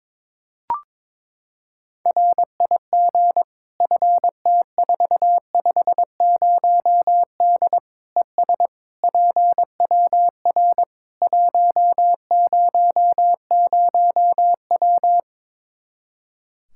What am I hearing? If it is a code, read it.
Morse code, heard as a keyed tone, about 22 wpm, 700 Hz. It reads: RIG FT450D ES PWR 100W
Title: K1USN SST QSO Practice 22wpm